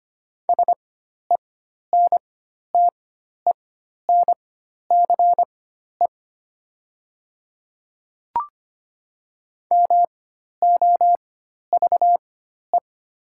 Morse code, heard as a keyed tone, about 25 wpm, spaced as for 12 wpm, 690 Hz. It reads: SENTENCE MOVE
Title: 4x Character Spacing - Top 300 Words - 25wpm